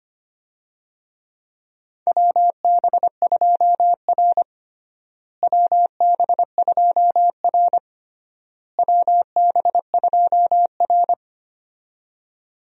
Morse code, heard as a keyed tone, about 25 wpm, 700 Hz. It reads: WB2R WB2R WB2R
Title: USA Call Signs - 4 Characters - Repeated 3x - 25wpm